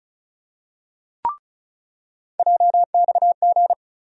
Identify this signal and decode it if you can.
Morse code, heard as a keyed tone, about 34 wpm, 700 Hz. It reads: JXG
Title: USA Call Sign Postfixes - 3 Letters 35wpm